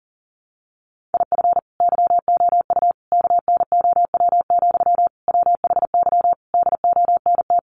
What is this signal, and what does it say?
Morse code, heard as a keyed tone, about 40 wpm, 700 Hz. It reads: IF YOU KNOW, WHY DONT
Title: Sentences from Top 300 Words 40wpm